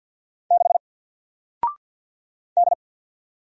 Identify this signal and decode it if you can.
Morse code, heard as a keyed tone, about 48 wpm, 700 Hz. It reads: 6 D
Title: Single Character (Letter:Number:Punctuation:Prosign) Alt 50wpm